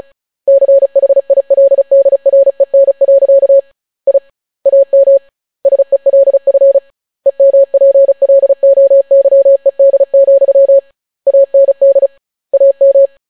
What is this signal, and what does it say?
Morse code, heard as a keyed tone, about 36 wpm, 550 Hz. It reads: CHILDREN. I AM SELF EMPLOYED, AND AM